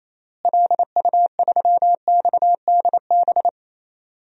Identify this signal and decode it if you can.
Morse code, heard as a keyed tone, about 28 wpm, 700 Hz. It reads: LU3XDB
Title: DXCC-100 Call Signs - 6 Characters - Repeated 3x - 28wpm